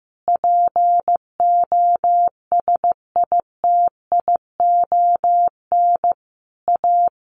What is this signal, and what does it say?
Morse code, heard as a keyed tone, about 15 wpm, 700 Hz. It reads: POSITION A
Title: Sentences from Top 1200 Words 15wpm